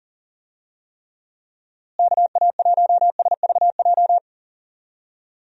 Morse code, heard as a keyed tone, about 40 wpm, 700 Hz. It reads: KA1SVJ
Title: USA Call Signs - 6 Characters - Repeated 2x - 40wpm